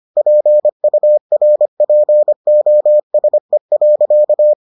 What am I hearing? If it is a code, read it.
Morse code, heard as a keyed tone, about 25 wpm, 600 Hz. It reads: PURPOSE.